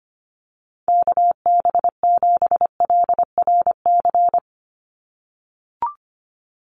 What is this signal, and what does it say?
Morse code, heard as a keyed tone, about 25 wpm, 700 Hz. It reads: KB7LRC
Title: USA Call Signs 25wpm